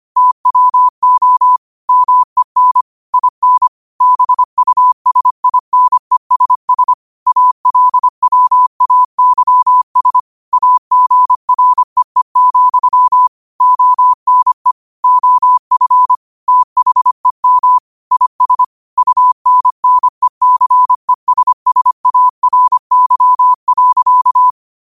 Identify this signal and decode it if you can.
Morse code, heard as a keyed tone, about 25 wpm, 1000 Hz. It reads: TWO MEN IN BUSINESS ALWAYS AGREE, ONE OF THEM IS UNNECESSARY.
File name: qotdfiles / 25wpm